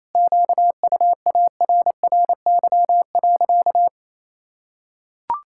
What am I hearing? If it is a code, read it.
Morse code, heard as a keyed tone, about 28 wpm, 700 Hz. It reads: QUARRY.
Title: Pangram Challenge v2 - 28wpm